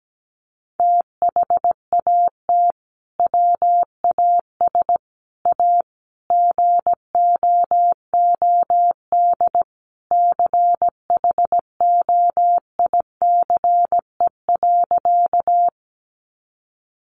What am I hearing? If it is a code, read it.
Morse code, heard as a keyed tone, about 17 wpm, 700 Hz. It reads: THAT WAS A GOOD CHOICE.
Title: Sentences from Top 900 Words 17wpm